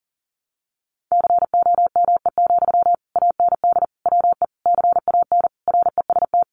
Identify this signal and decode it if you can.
Morse code, heard as a keyed tone, about 40 wpm, 700 Hz. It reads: COME, AND WE CAN REST
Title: Sentences from Top 500 Words 40wpm